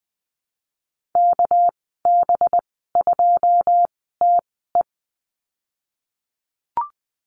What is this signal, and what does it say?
Morse code, heard as a keyed tone, about 20 wpm, spaced as for 15 wpm, 700 Hz. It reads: KB2TE
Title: 207 Lesson 33g - Letter j - Call Signs 20wpm